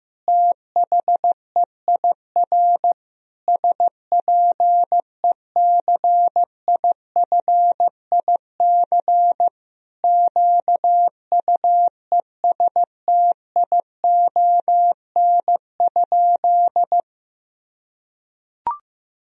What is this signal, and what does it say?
Morse code, heard as a keyed tone, about 15 wpm, 700 Hz. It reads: THEIR SPECIFIC QUESTION?
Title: Sentences from Top 1000 Words - Review Part 3 - 15wpm